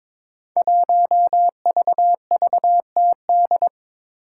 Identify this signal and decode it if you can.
Morse code, heard as a keyed tone, about 22 wpm, 700 Hz. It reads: 1VVTD
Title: Five Challenging Letter-Number Combinations 22wpm